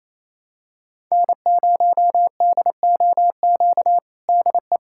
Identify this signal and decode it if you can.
Morse code, heard as a keyed tone, about 28 wpm, 700 Hz. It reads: N0DOQ DE
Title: USA Callsign DE Callsign 28wpm